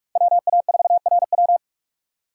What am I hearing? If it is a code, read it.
Morse code, heard as a keyed tone, about 46 wpm, 690 Hz. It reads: WA4RW
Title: USA Call Signs - 5 Characters - 2x Character Spacing - 45wpm